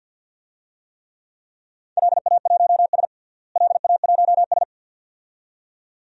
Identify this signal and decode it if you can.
Morse code, heard as a keyed tone, about 53 wpm, 700 Hz. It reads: LA1S LA1S
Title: DXCC-100 Call Signs - 4 Characters - Repeated 2x - 50wpm